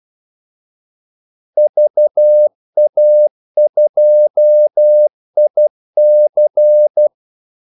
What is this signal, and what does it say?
Morse code, heard as a keyed tone, about 12 wpm, 600 Hz. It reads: VA2IC
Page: 2 ekstra ordmellemrum mellem kaldesignalerne